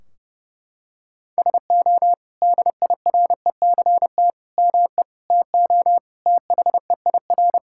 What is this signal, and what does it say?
Morse code, heard as a keyed tone, about 32 wpm, 700 Hz. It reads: SO DIRECT ME TO THEIR
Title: Sentences from Top 1600 Words 30wpm